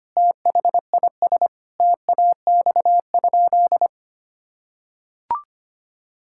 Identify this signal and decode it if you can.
Morse code, heard as a keyed tone, about 25 wpm, 700 Hz. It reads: THIS TAX?
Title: Sentences from Top 1000 Words - Review Part 5 - 25wpm